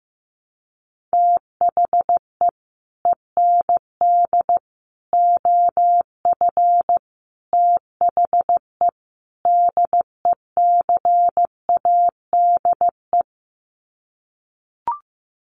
Morse code, heard as a keyed tone, about 15 wpm, 700 Hz. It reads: THE END OF THE DECADE
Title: Sets of 5 Words Encore 15wpm